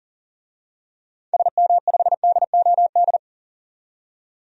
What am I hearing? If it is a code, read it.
Morse code, heard as a keyed tone, about 40 wpm, 700 Hz. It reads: SM5DOD
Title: DXCC-100 Call Signs - 6 Characters - 40wpm